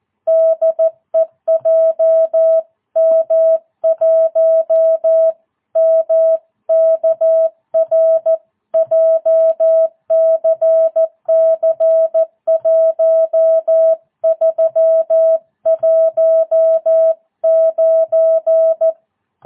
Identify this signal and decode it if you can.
Morse code, heard as a keyed tone, about 14 wpm, 650 Hz. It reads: DE JM1MKR JCC 1319